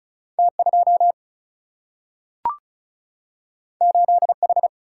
Morse code, heard as a keyed tone, about 36 wpm, 700 Hz. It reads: T 2 8 H
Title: Single-Letter Single-Number 35wpm